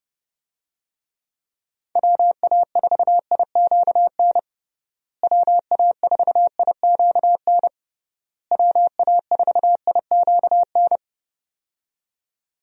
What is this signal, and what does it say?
Morse code, heard as a keyed tone, about 30 wpm, 700 Hz. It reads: WA4IQN WA4IQN WA4IQN
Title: USA Call Signs - 6 Characters - Repeated 3x - 30wpm